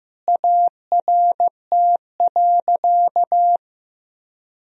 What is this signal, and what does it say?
Morse code, heard as a keyed tone, about 15 wpm, 700 Hz. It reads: ART.
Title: Sentences from Top 1000 Words - Review Part 3 - 15wpm